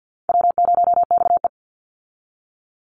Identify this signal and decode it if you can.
Morse code, heard as a keyed tone, about 50 wpm, 700 Hz. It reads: W0XE
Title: Easy Call Signs - 50wpm